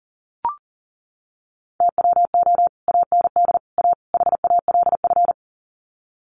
Morse code, heard as a keyed tone, about 40 wpm, 700 Hz. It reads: TWO AND A HALF
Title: Sets of 4 Words 40wpm